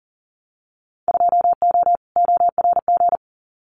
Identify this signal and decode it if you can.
Morse code, heard as a keyed tone, about 40 wpm, 700 Hz. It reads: 2O ORG
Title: Winter Field Day Exchanges 40wpm